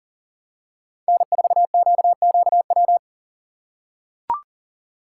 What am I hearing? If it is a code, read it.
Morse code, heard as a keyed tone, about 37 wpm, 700 Hz. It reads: N4QQW
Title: USA Call Signs - 6x Character Spacing - 40wpm